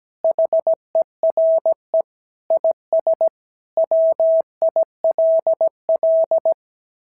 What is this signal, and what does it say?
Morse code, heard as a keyed tone, about 17 wpm, 690 Hz. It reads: HERE IS WILL
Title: SOTA QSO Practice 17wpm